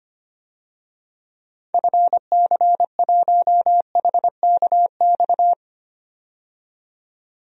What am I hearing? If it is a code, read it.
Morse code, heard as a keyed tone, about 25 wpm, 700 Hz. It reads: FC1HKX
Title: DXCC-100 Call Signs - 6 Characters - 25wpm